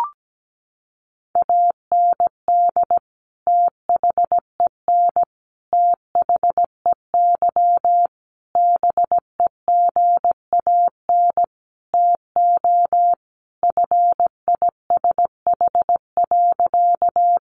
Morse code, heard as a keyed tone, about 17 wpm, 700 Hz. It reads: AND THEN THEY BEGAN TO FISH.
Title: Sentences from Top 300 Words 17wpm